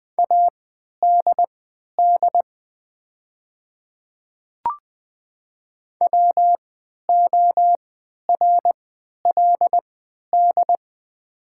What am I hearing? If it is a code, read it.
Morse code, heard as a keyed tone, about 20 wpm, spaced as for 11 wpm, 700 Hz. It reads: ADD WORLD
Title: 3x Character Spacing - Top 200 Words - 20wpm